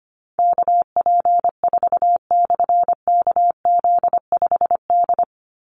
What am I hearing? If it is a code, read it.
Morse code, heard as a keyed tone, about 25 wpm, 700 Hz. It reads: KP4/KZ5D
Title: Call Signs Repeated 3x - 25wpm